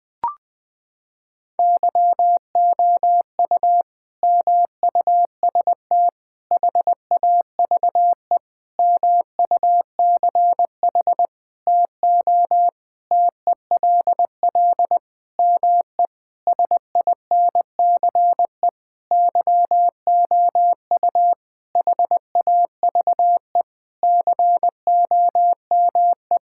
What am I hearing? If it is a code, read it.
Morse code, heard as a keyed tone, about 21 wpm, 700 Hz. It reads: YOU MUST HAVE MUCH TO TELL ME SINCE YOU HAVE COME
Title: Sentences from Top 1600 Words 20wpm